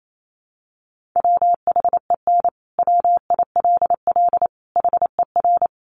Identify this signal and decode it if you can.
Morse code, heard as a keyed tone, about 28 wpm, 700 Hz. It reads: WHEN WILL HER
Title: Sentences from Top 200 Words 28wpm